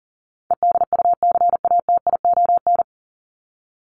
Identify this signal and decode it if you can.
Morse code, heard as a keyed tone, about 40 wpm, 700 Hz. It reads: EDUCATION
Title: Top 751-1000 Words 40wpm